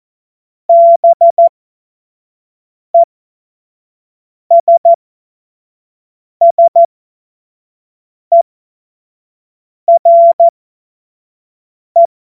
Morse code, heard as a keyed tone, about 14 wpm, spaced as for 5 wpm, 680 Hz. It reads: BESSERE
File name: Morsekurs / Witze / Witz 06. Juli 2021, 25 F70